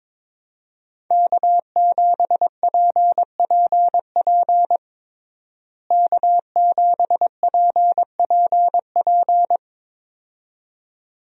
Morse code, heard as a keyed tone, about 22 wpm, 700 Hz. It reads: K7PPP K7PPP
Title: USA Call Signs - 5 Characters - Repeated 2x - 22wpm